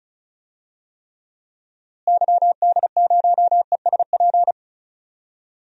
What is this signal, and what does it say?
Morse code, heard as a keyed tone, about 34 wpm, 700 Hz. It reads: YD0ESP
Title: DXCC-100 Call Signs - 6 Characters - Repeated 2x - 35wpm